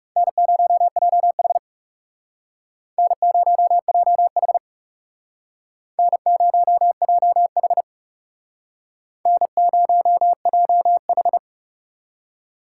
Morse code, heard as a keyed tone, about 38 wpm, 700 Hz. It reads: N0JH N0JH N0JH N0JH